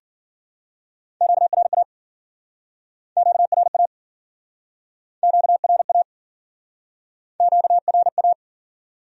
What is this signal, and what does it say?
Morse code, heard as a keyed tone, about 54 wpm, 700 Hz. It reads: QRA QRA QRA QRA